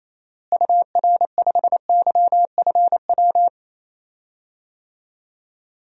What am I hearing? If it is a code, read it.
Morse code, heard as a keyed tone, about 27 wpm, 700 Hz. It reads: UR5YFW
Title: DXCC-100 Call Signs - 6 Characters - Repeated 2x - 28wpm